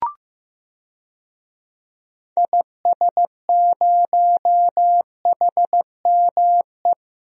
Morse code, heard as a keyed tone, about 15 wpm, 690 Hz. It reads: IS0HME
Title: DXCC-100 Call Signs - 6 Characters - 15wpm